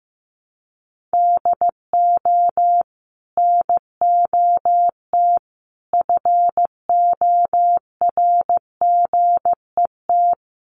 Morse code, heard as a keyed tone, about 15 wpm, 700 Hz. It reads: DO NOT FORGET
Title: Sentences from Top 1200 Words 15wpm